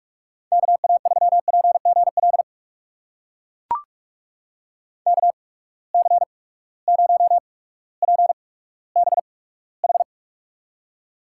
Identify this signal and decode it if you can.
Morse code, heard as a keyed tone, about 44 wpm, spaced as for 19 wpm, 700 Hz. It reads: KA3PGL KC0PBH
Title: USA Call Signs - 6 Characters - 8x Character Spacing - 45wpm